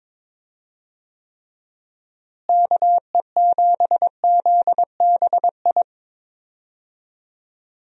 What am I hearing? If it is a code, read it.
Morse code, heard as a keyed tone, about 22 wpm, 700 Hz. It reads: KE7ZBI
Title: USA Call Signs - 6 Characters - Repeated 2x - 22wpm